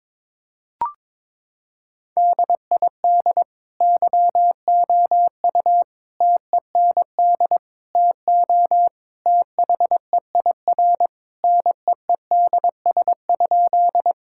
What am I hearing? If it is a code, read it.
Morse code, heard as a keyed tone, about 22 wpm, 700 Hz. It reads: DID YOU TEND TO THEIR NEEDS?
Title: Sentences from Top 1000 Words - Review Part 5 - 22wpm